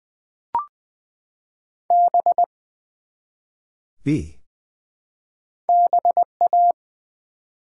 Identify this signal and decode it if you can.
Morse code, heard as a keyed tone, about 20 wpm, 700 Hz. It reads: B BA
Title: Build A Word - 1 Letter At A Time - 3-Letter Words - 20wpm